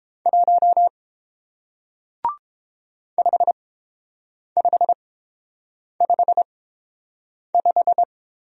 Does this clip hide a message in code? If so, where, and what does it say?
Yes: Morse code from 0.3 to 8.0 s, heard as a keyed tone, about 30 wpm, spaced as for 7 wpm, 700 Hz. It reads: 1 5555